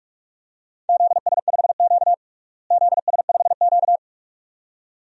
Morse code, heard as a keyed tone, about 46 wpm, 700 Hz. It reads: ZS5Q ZS5Q
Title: DXCC-100 Call Signs - 4 Characters - Repeated 2x - 45wpm